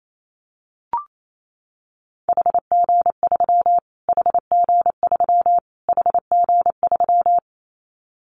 Morse code, heard as a keyed tone, about 28 wpm, 700 Hz. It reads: HG3 HG3 HG3
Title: Callsign Prefixes for Top 25 DXCC - Repeated 3x - 28wpm